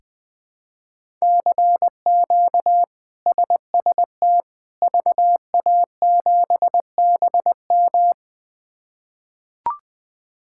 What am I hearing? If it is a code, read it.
Morse code, heard as a keyed tone, about 20 wpm, 700 Hz. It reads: CQ SST VA7BM
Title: K1USN SST QSO Practice - 20wpm